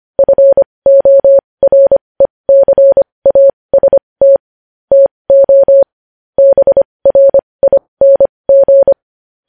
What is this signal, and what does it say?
Morse code, heard as a keyed tone, about 25 wpm, spaced as for 20 wpm, 550 Hz. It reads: FORECAST TO BRING
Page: News Headlines in Morse Code at 20 WPM